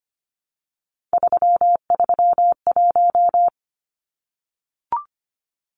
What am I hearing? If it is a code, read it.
Morse code, heard as a keyed tone, about 25 wpm, 700 Hz. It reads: 331
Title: Numbers - 3-Digit 25wpm